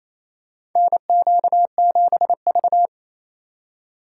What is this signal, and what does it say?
Morse code, heard as a keyed tone, about 28 wpm, 700 Hz. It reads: NQ7V
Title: USA Call Signs - 4 Characters - 3x Character Spacing - 28wpm